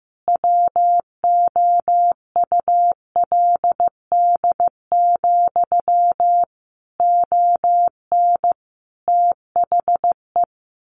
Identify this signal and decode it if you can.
Morse code, heard as a keyed tone, about 15 wpm, 700 Hz. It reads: WOULD, ON THE
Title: A Sound Of Thunder 15wpm